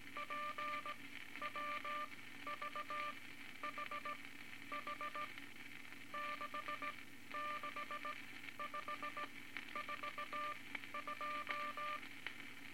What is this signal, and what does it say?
Morse code, heard as a keyed tone, about 18 wpm, spaced as for 12 wpm, 1190 Hz, with noise. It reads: PWVHH 66542
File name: Morse Code Records / 18 A LCM 164 Ten Words Per Minute